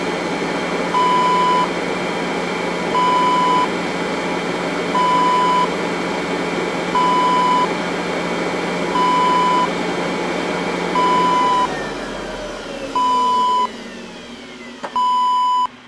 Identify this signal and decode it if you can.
There is no code.